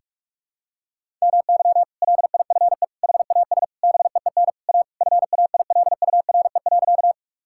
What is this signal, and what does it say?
Morse code, heard as a keyed tone, about 43 wpm, 700 Hz. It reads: MY LIFE HAS BEEN A FAILURE.
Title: Sentences from Top 1000 Words - Review Part 6 - 45wpm